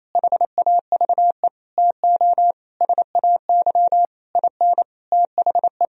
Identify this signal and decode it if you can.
Morse code, heard as a keyed tone, about 28 wpm, 700 Hz. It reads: HAVE TO SAY IN THE
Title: Sentences from Top 1000 Words - Review Part 3 - 28wpm